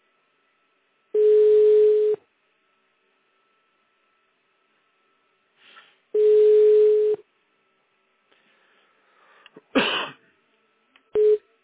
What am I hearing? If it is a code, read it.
Ringback tone (Europe, 425 Hz).